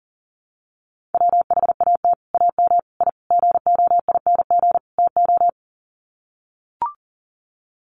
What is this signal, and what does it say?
Morse code, heard as a keyed tone, about 40 wpm, 700 Hz. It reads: WHAT AM I GOING TO
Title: Sets of 5 Words Encore 40wpm